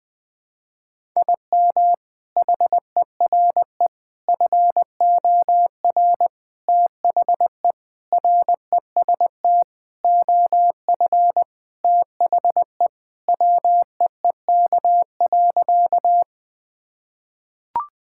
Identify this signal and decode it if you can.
Morse code, heard as a keyed tone, about 21 wpm, 700 Hz. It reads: IM HERE FOR THE REST OF THE WEEK.
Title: Sentences with 4-Word Phrases - Spoken Beginning - 20wpm